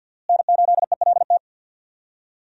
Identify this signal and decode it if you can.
Morse code, heard as a keyed tone, about 49 wpm, 700 Hz. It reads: N8ELT
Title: USA Call Signs - 5 Characters - Repeated 3x - 50wpm